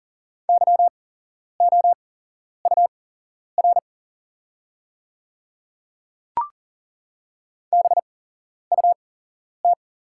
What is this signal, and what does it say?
Morse code, heard as a keyed tone, about 40 wpm, spaced as for 11 wpm, 690 Hz. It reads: YOUR BUT
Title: 8x Character Spacing - Top 200 Words - 40wpm